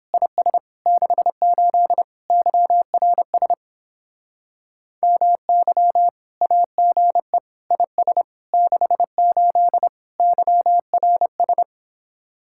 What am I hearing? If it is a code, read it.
Morse code, heard as a keyed tone, about 31 wpm, 700 Hz. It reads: IS 68 YRS MY AGE IS 68 YRS